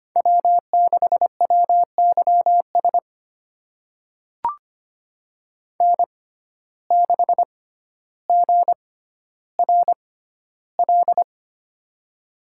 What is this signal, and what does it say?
Morse code, heard as a keyed tone, about 25 wpm, spaced as for 13 wpm, 700 Hz. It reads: W6WYS N6GRL
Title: USA Call Signs - 5 Characters - 6x Character Spacing - 25wpm